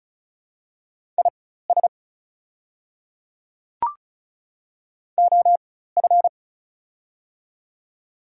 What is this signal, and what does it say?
Morse code, heard as a keyed tone, about 36 wpm, spaced as for 16 wpm, 700 Hz. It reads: IS OF
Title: 4x Character Spacing - Top 100 Words - 35wpm